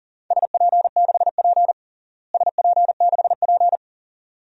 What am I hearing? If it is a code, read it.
Morse code, heard as a keyed tone, about 40 wpm, 700 Hz. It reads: SP6P SP6P
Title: DXCC-100 Call Signs - 4 Characters - Repeated 2x - 40wpm